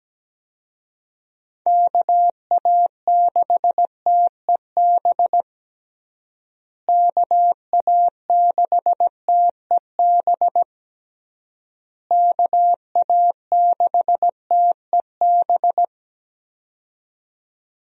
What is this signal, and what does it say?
Morse code, heard as a keyed tone, about 17 wpm, 700 Hz. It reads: KA6TEB KA6TEB KA6TEB
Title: USA Call Signs - 6 Characters - Repeated 3x - 17wpm